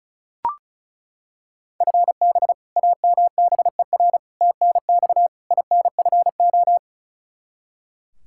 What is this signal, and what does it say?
Morse code, heard as a keyed tone, about 36 wpm, 700 Hz. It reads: FB AMBER TNX INFO
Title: Practice QSO - Exchange 3 - More Info - 35wpm